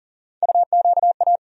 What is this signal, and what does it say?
Morse code, heard as a keyed tone, about 40 wpm, 700 Hz. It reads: UQA
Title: USA Call Sign Postfixes - 3 Letters 40wpm